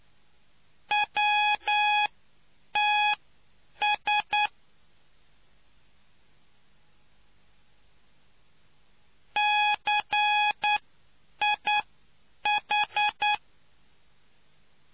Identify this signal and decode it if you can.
Morse code, heard as a keyed tone, about 9 wpm, spaced as for 7 wpm, 840 Hz. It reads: WTS CIH